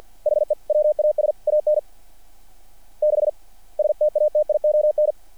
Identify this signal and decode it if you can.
Morse code, heard as a keyed tone, about 48 wpm, 660 Hz. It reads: HE WAS AN 6 STATION
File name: archived / CRSnet